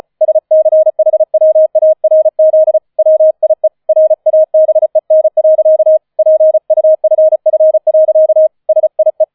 Morse code, heard as a keyed tone, about 35 wpm, 630 Hz. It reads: SCHWARZ WIE RABEN. PUFF. SIE